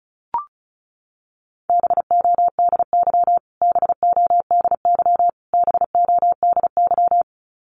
Morse code, heard as a keyed tone, about 34 wpm, 700 Hz. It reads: BODY BODY BODY
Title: Top 500 Words Repeated 35wpm